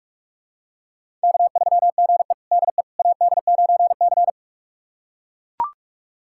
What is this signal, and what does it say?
Morse code, heard as a keyed tone, about 42 wpm, 700 Hz. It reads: K3GE DE AD9C
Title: USA Callsign DE Callsign - 4 Characters - 45wpm